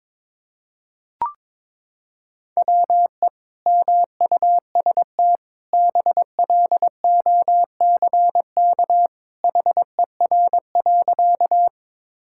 Morse code, heard as a keyed tone, about 22 wpm, 700 Hz. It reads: WE MUST BLOCK HER.
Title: Sentences from Top 1600 Words 22wpm